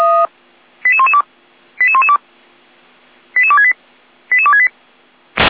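Five-tone selective call (ZVEI), heard as quick runs of five tones, tones of about 70 ms. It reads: 88112 88112 88268 88268
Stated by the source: ZVEI five-tone selective call received by SDR.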